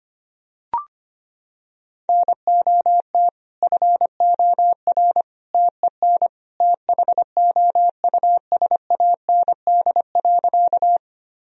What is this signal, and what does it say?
Morse code, heard as a keyed tone, about 25 wpm, 700 Hz. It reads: NOT FOR TEN THOUSAND.
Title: Sentences from Top 1000 Words - Review Part 3 - 25wpm